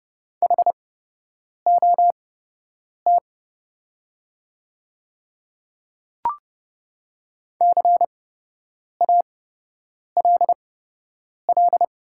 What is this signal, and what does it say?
Morse code, heard as a keyed tone, about 30 wpm, spaced as for 8 wpm, 700 Hz. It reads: HOT CALL
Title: 8x Character Spacing - Top 200 Words - 30wpm